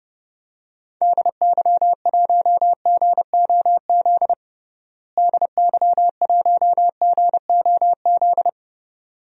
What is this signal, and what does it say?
Morse code, heard as a keyed tone, about 30 wpm, 700 Hz. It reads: DY1GOZ DY1GOZ
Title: DXCC-100 Call Signs - 6 Characters - Repeated 2x - 30wpm